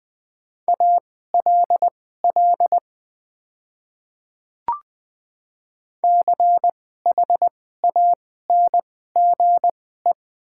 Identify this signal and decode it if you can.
Morse code, heard as a keyed tone, about 20 wpm, spaced as for 14 wpm, 700 Hz. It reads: ALL CHANGE
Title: 2x Character Spacing - Top 200 Words - 20wpm